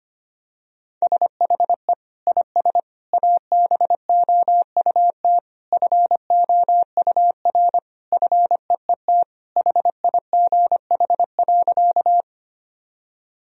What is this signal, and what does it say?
Morse code, heard as a keyed tone, about 25 wpm, 700 Hz. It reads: SHE IS ABOUT FOUR FEET HIGH.
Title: Sentences from Top 1000 Words - Review Part 1 - 25wpm